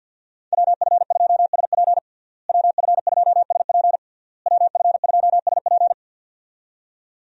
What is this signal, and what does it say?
Morse code, heard as a keyed tone, about 52 wpm, 700 Hz. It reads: WF2SP WF2SP WF2SP
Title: USA Call Signs - 5 Characters - Repeated 3x - 50wpm